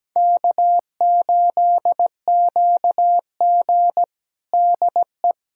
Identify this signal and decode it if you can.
Morse code, heard as a keyed tone, about 17 wpm, 700 Hz. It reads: K8QG DE
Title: USA Callsign DE Callsign - 4 Characters - 17wpm